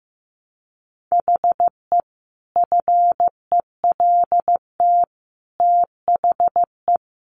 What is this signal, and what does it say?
Morse code, heard as a keyed tone, about 15 wpm, 700 Hz. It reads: HE FELT THE
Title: A Sound Of Thunder 15wpm